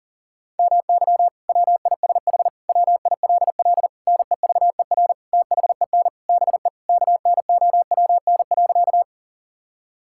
Morse code, heard as a keyed tone, about 37 wpm, 700 Hz. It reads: MY WISH WILL NEVER THEN BE KNOWN.
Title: Sentences from Top 1600 Words 40wpm